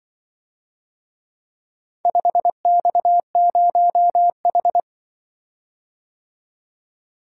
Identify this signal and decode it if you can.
Morse code, heard as a keyed tone, about 24 wpm, 700 Hz. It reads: 5X0H
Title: Speed-Racing - Four Letter-Number Combinations - 20wpm